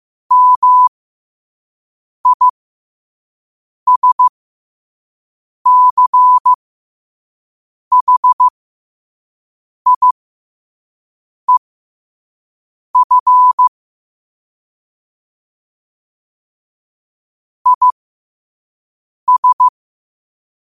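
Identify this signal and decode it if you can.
Morse code, heard as a keyed tone, about 15 wpm, spaced as for 5 wpm, 1000 Hz. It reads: MISCHIEF IS
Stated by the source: Morse code quote of the day sent at 5 WPM.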